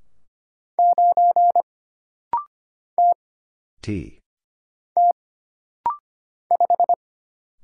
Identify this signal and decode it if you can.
Morse code, heard as a keyed tone, about 25 wpm, 700 Hz. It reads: 9 T T 5
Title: Single Letter-Number - Rapid-Fire 25wpm